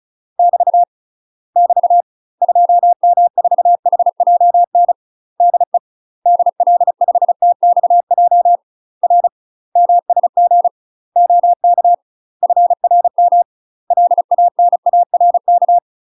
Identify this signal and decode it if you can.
Morse code, heard as a keyed tone, about 36 wpm, 700 Hz. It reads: <BT> <BT> 2M4HJN DE DL5TXJ R MSG OK FRM LANARK
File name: UK MASTER QSOs TRAINING / 35WPM / UK MASTER QSOs 0011